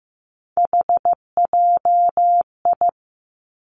Morse code, heard as a keyed tone, about 15 wpm, 700 Hz. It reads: HJI
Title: Three Challenging Letter-Number Combinations 15wpm